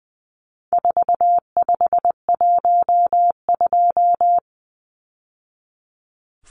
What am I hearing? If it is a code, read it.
Morse code, heard as a keyed tone, about 20 wpm, 700 Hz. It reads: 4512
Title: Numbers - 4-Digit 20wpm